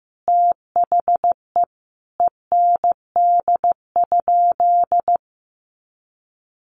Morse code, heard as a keyed tone, about 15 wpm, 700 Hz. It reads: THE END?
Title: Sentences from Top 800 Words 15wpm